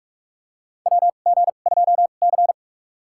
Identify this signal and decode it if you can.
Morse code, heard as a keyed tone, about 46 wpm, spaced as for 33 wpm, 700 Hz. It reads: WG2C
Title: USA Call Signs - 2x Character Spacing - 45wpm